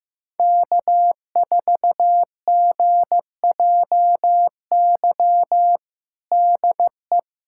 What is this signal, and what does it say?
Morse code, heard as a keyed tone, about 15 wpm, 700 Hz. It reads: K4GJY DE
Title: USA Callsign DE Callsign - 5 Characters - 15wpm